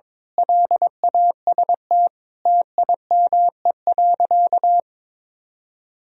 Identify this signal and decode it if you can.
Morse code, heard as a keyed tone, about 21 wpm, 700 Hz. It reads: LAST TIME.
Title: Sentences from Top 1000 Words - Review Part 2 - 22wpm